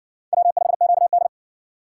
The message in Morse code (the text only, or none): W5QD